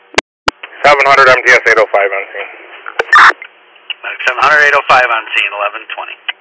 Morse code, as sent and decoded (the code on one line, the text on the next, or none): none